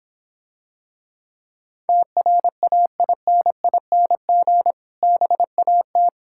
Morse code, heard as a keyed tone, about 25 wpm, 700 Hz. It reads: TRAINING BAT